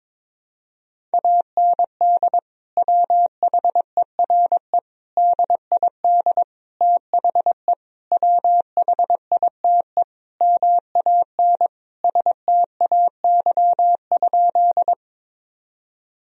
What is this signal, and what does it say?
Morse code, heard as a keyed tone, about 22 wpm, 700 Hz. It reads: AND WHERE DID THE WHITE MAN STAY?
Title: Sentences from Top 1000 Words - Review Part 1 - 22wpm